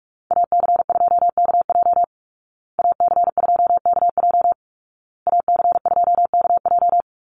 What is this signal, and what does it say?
Morse code, heard as a keyed tone, about 44 wpm, 700 Hz. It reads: AC2KJ AC2KJ AC2KJ
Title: Call Signs Repeated 2x - 45wpm